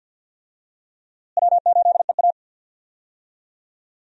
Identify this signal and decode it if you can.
Morse code, heard as a keyed tone, about 52 wpm, 700 Hz. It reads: W8EA
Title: USA Call Signs - 4 Characters - Repeated 3x - 50wpm